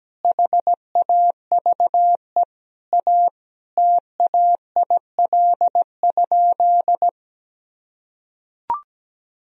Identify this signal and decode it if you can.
Morse code, heard as a keyed tone, about 17 wpm, 700 Hz. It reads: HAVE A TAIL?
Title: Sentences from Top 1000 Words - Review Part 3 - 17wpm